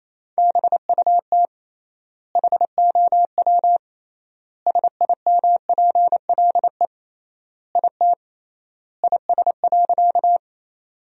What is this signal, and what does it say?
Morse code, heard as a keyed tone, about 28 wpm, 700 Hz. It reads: BUT HOW SIMPLE IT IS.
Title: 3x Word-Spacing - Sentences from Top 400 Words - 28wpm